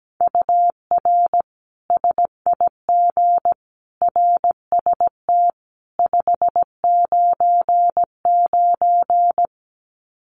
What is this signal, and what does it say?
Morse code, heard as a keyed tone, about 17 wpm, 700 Hz. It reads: UR SIG RST 599
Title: Common QSO Phrases 17wpm